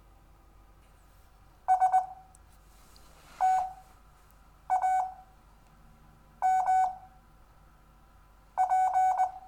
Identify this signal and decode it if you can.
Morse code, heard as a keyed tone, about 20 wpm, spaced as for 6 wpm, 750 Hz. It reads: STAMP
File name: cw / Lesson4 REVIEW